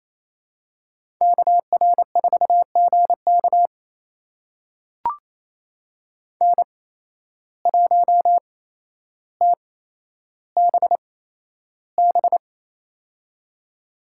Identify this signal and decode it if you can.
Morse code, heard as a keyed tone, about 28 wpm, spaced as for 12 wpm, 700 Hz. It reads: KR4GK N1TBB
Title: USA Call Signs - 5 Characters - 8x Character Spacing - 28wpm